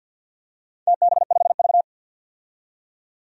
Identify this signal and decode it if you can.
Morse code, heard as a keyed tone, about 53 wpm, 700 Hz. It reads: TB5V
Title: DXCC-100 Call Signs - 4 Characters - 50wpm